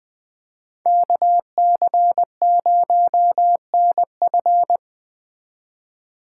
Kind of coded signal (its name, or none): Morse code